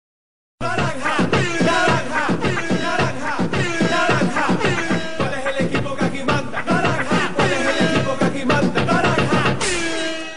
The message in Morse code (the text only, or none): none